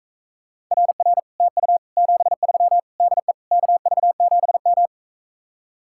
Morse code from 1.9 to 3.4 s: 73 DE

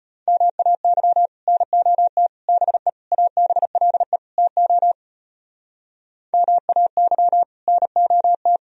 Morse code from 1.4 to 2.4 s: NOT

